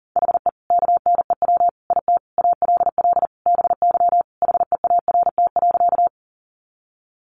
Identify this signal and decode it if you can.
Morse code, heard as a keyed tone, about 40 wpm, 700 Hz. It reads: HE KNEW IT ALL BY HEART.
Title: Sentences from Top 1200 Words 40wpm